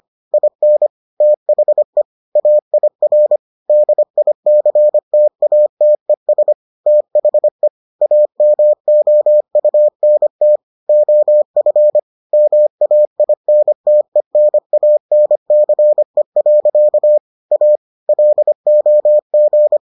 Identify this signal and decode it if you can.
Morse code, heard as a keyed tone, about 25 wpm, 600 Hz. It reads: IN THE AIR DICTATES THE AMOUNT OF MAINTENANCE. A LOG